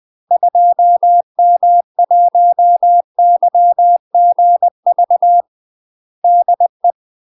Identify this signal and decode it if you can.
Morse code, heard as a keyed tone, about 20 wpm, 700 Hz. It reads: 2M1YGV DE
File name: UK MASTER QSOs TRAINING / 20WPM / UK MASTER QSOs 0048